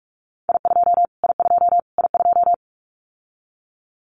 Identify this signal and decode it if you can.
Morse code, heard as a keyed tone, about 44 wpm, 700 Hz. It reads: I2 I2 I2
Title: Callsign Prefixes for Top 25 DXCC - Repeated 3x - 45wpm